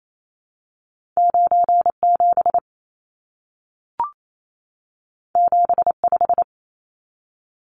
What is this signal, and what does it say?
Morse code, heard as a keyed tone, about 28 wpm, 700 Hz. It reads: 97 75
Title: Numbers - 2-Digit 28wpm